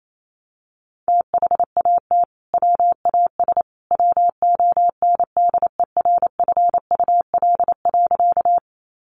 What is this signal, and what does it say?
Morse code, heard as a keyed tone, about 28 wpm, 700 Hz. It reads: THAT WAS WONDERFUL.
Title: Sentences from Top 1100 Words 28wpm